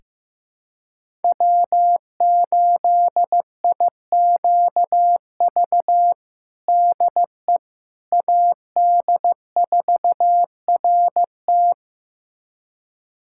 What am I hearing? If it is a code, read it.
Morse code, heard as a keyed tone, about 15 wpm, 700 Hz. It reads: W8IQV DE AD4RT
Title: USA Callsign DE Callsign 15wpm